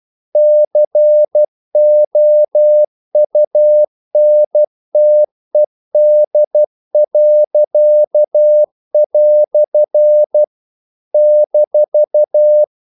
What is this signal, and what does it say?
Morse code, heard as a keyed tone, about 12 wpm, 600 Hz. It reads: COUNTED." -
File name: CW / Practice / EnglishQuotations0112